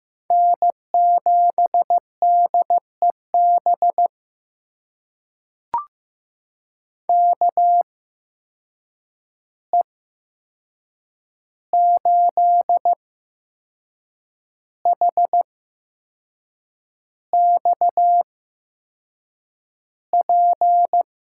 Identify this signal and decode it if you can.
Morse code, heard as a keyed tone, about 15 wpm, spaced as for 6 wpm, 700 Hz. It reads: N7DEB KE8HXP
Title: USA Call Signs - 8x Character Spacing - 15wpm